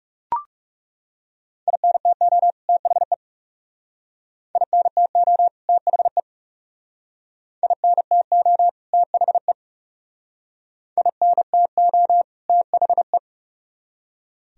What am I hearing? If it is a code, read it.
Morse code, heard as a keyed tone, about 38 wpm, 700 Hz. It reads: INTO THE INTO THE INTO THE INTO THE